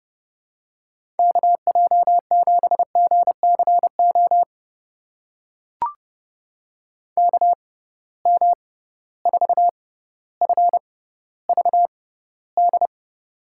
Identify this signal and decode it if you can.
Morse code, heard as a keyed tone, about 30 wpm, spaced as for 15 wpm, 700 Hz. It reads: KJ7GCO KM4FVD
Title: USA Call Signs - 6 Characters - 6x Character Spacing - 30wpm